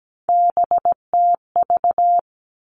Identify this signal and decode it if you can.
Morse code, heard as a keyed tone, about 17 wpm, 700 Hz. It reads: BTV
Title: All Cut Numbers - 3-Digit 17wpm